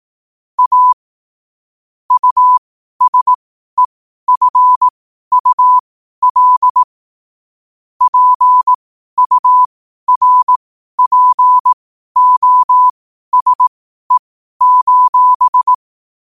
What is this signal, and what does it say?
Morse code, heard as a keyed tone, about 18 wpm, spaced as for 12 wpm, 1000 Hz. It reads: A USEFUL PURPOSE:
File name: qotdfiles / 12wpm